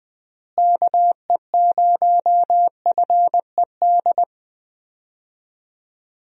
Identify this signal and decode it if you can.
Morse code, heard as a keyed tone, about 20 wpm, 700 Hz. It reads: KE0FED
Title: USA Call Signs - 6 Characters - Repeated 3x - 20wpm